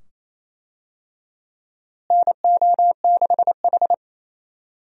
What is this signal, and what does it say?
Morse code, heard as a keyed tone, about 28 wpm, 700 Hz. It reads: NO6H
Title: USA Call Signs - 4 Characters - Repeated 2x - 28wpm